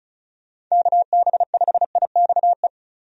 Morse code, heard as a keyed tone, about 34 wpm, 700 Hz. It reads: KB5IXE
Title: USA Call Signs - 6 Characters - 35wpm